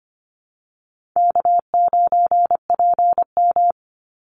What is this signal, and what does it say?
Morse code, heard as a keyed tone, about 25 wpm, 700 Hz. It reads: K9PM
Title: NA SOTA Chaser Callsigns 25wpm